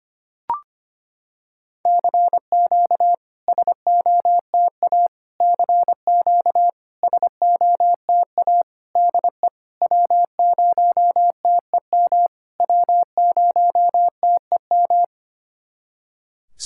Morse code, heard as a keyed tone, about 25 wpm, 700 Hz. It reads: CQ SOTA CQ SOTA DE W0TEM W0TEM
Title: SOTA QSO Practice 25wpm